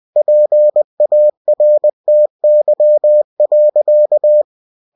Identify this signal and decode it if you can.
Morse code, heard as a keyed tone, about 20 wpm, 600 Hz. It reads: PARTY.
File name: cw-books / winnie-the-pooh / winnie-20wpm-page-0102